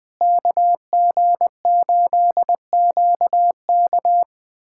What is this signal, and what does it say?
Morse code, heard as a keyed tone, about 20 wpm, 700 Hz. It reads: KG8QK
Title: USA Callsign DE Callsign - 5 Characters - 20wpm